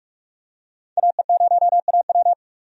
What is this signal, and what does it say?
Morse code, heard as a keyed tone, about 46 wpm, 700 Hz. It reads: AE0AW
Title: USA Call Signs - 5 Characters - 45wpm